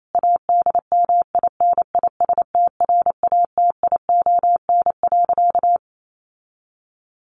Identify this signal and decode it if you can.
Morse code, heard as a keyed tone, about 28 wpm, 700 Hz. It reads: ADMINISTRATION.
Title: Sentences from Top 1300 Words 28wpm